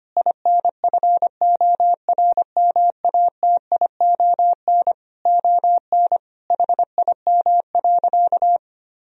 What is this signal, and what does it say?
Morse code, heard as a keyed tone, about 25 wpm, 700 Hz. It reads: INFORMATION ON HIM.
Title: Sentences from Top 1000 Words - Review Part 3 - 25wpm